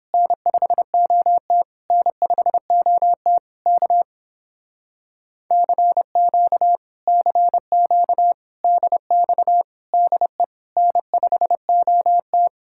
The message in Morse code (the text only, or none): N5OT N5OT K CQ CQ DX DE N5OT